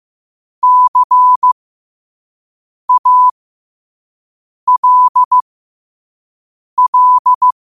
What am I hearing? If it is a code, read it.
Morse code, heard as a keyed tone, about 15 wpm, spaced as for 5 wpm, 1000 Hz. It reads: CALL
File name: qotdfiles / 5wpm